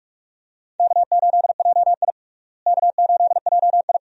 Morse code, heard as a keyed tone, about 46 wpm, 700 Hz. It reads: K8JI K8JI
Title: USA Call Signs - 4 Characters - Repeated 3x - 45wpm